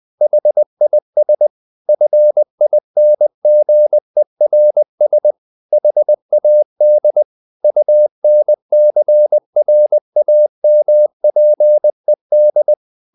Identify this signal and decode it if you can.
Morse code, heard as a keyed tone, about 20 wpm, 600 Hz. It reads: HIS FINGERS HAD UNCRAMPED
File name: the old man and the sea / the old man and the sea part 1 20wpm